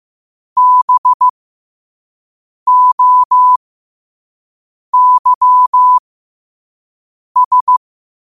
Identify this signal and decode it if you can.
Morse code, heard as a keyed tone, about 15 wpm, spaced as for 5 wpm, 1000 Hz. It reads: BOYS